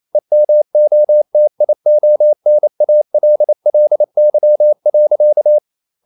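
Morse code, heard as a keyed tone, about 28 wpm, 600 Hz. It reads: EMOTIONALLY.
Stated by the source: Morse code practice at 28 wpm, text of Ken Follett's novel Never in English, from CW med Gnister.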